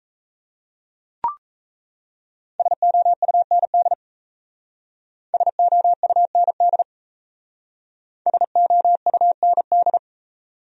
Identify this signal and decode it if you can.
Morse code, heard as a keyed tone, about 36 wpm, 700 Hz. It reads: SOUND SOUND SOUND